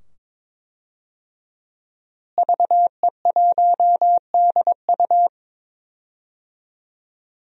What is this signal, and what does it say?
Morse code, heard as a keyed tone, about 22 wpm, 700 Hz. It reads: VE1DU
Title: DXCC-100 Call Signs - 5 Characters - Repeated 2x - 22wpm